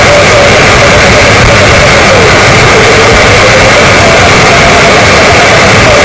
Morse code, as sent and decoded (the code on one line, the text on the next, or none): none